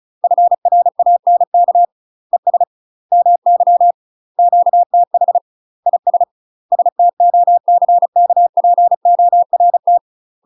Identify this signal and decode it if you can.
Morse code, heard as a keyed tone, about 34 wpm, 700 Hz. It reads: FRANK ES MY QTH IS STOCKPORT
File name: UK MASTER QSOs TRAINING/35WPM